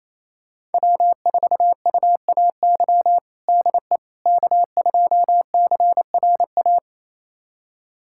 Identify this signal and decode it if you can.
Morse code, heard as a keyed tone, about 28 wpm, 690 Hz. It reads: W4UAY DE K2CRA
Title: USA Callsign DE Callsign - 5 Characters - 28wpm